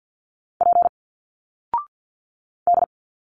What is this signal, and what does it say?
Morse code, heard as a keyed tone, about 50 wpm, 700 Hz. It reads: P D
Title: Single Letters 50wpm